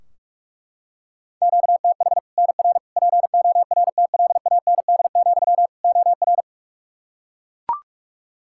Morse code, heard as a keyed tone, about 41 wpm, 700 Hz. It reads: QTH NR PORTLAND, OR
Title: Practice QSO - Exchange 1 - Beginning - 45wpm